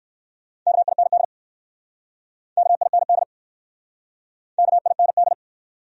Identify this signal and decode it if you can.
Morse code, heard as a keyed tone, about 59 wpm, 700 Hz. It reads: KIND KIND KIND